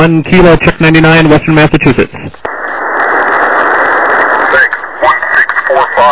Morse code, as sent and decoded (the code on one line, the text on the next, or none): none